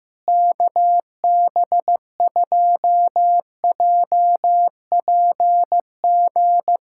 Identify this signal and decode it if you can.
Morse code, heard as a keyed tone, about 15 wpm, 700 Hz. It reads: KB2JPG
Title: USA Callsign DE Callsign - 6 Characters - 15wpm